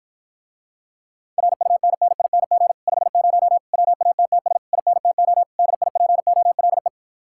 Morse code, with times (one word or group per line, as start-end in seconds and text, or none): 1.4-2.7 RUNNING
2.9-3.6 50
3.7-4.6 WATTS
4.7-5.4 INTO
5.6-6.9 DIPOLE